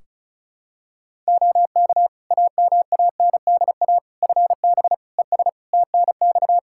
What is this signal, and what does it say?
Morse code, heard as a keyed tone, about 36 wpm, 700 Hz. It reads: OK AMANDA FB ES TNX
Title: K1USN SST QSO Practice 35wpm